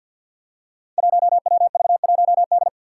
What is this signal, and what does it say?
Morse code, heard as a keyed tone, about 52 wpm, 700 Hz. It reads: 1WV1D
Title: Five Challenging Letter-Number Combinations 50wpm